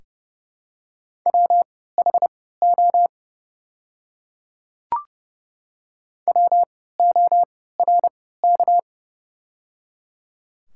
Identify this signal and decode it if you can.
Morse code, heard as a keyed tone, about 30 wpm, spaced as for 17 wpm, 700 Hz. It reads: WHO WORK
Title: 3x Character Spacing - Top 500 Words - 30wpm